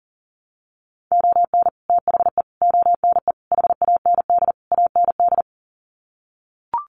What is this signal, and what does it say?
Morse code, heard as a keyed tone, about 40 wpm, 700 Hz. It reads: ON THE ONE HAND AND
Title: Sets of 5 Words Encore2 40wpm